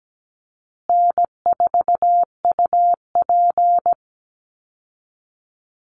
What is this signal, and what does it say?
Morse code, heard as a keyed tone, about 17 wpm, 700 Hz. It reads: N4UP
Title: Call Signs Repeated 2x - 17wpm